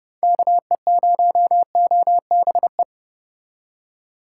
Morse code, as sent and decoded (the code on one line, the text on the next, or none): -.- . ----- --- -... .
KE0OBE